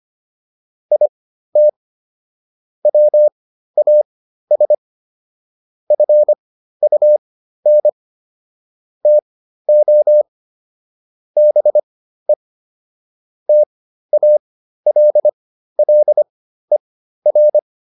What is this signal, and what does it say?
Morse code, heard as a keyed tone, about 25 wpm, spaced as for 13 wpm, 600 Hz. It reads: IT WAS FUN TO BE TALLER